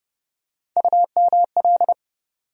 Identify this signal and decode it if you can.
Morse code, heard as a keyed tone, about 30 wpm, 700 Hz. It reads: UML
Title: USA Call Sign Postfixes - 3 Letters 30wpm